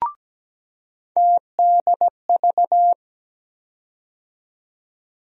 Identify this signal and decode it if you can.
Morse code, heard as a keyed tone, about 17 wpm, 700 Hz. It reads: TDV
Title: USA Call Sign Postfixes - 3 Letters 17wpm